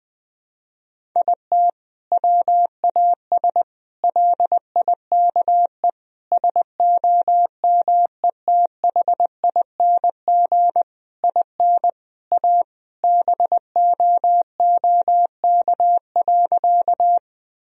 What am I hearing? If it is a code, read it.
Morse code, heard as a keyed tone, about 20 wpm, 700 Hz. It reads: IT WAS LIKE SOMETHING IN A BOOK.
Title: Sentences from Top 1000 Words - Review Part 1 - 20wpm